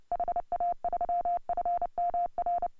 Morse code, heard as a keyed tone, about 29 wpm, 690 Hz. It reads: HA3FMR
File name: AUDIO2018 / HA3FMR CW